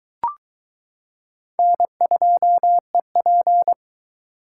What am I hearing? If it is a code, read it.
Morse code, heard as a keyed tone, about 22 wpm, 700 Hz. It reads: N2EP